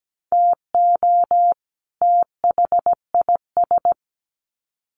Morse code, heard as a keyed tone, about 17 wpm, 700 Hz. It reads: TO THIS
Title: Sets of 2 Words 17wpm